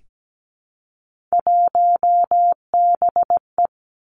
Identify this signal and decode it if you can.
Morse code, heard as a keyed tone, about 17 wpm, 700 Hz. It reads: 1BE
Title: Three Challenging Letter-Number Combinations 17wpm